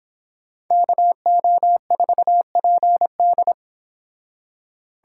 Morse code, heard as a keyed tone, about 29 wpm, 700 Hz. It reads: KO4PD